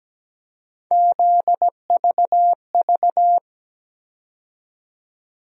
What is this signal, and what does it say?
Morse code, heard as a keyed tone, about 17 wpm, 700 Hz. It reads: ZVV
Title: USA Call Sign Postfixes - 3 Letters 17wpm